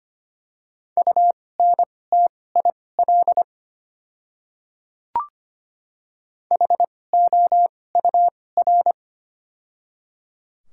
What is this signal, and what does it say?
Morse code, heard as a keyed tone, about 25 wpm, spaced as for 18 wpm, 700 Hz. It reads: UNTIL HOUR
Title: 2x Character Spacing - Top 500 Words - 25wpm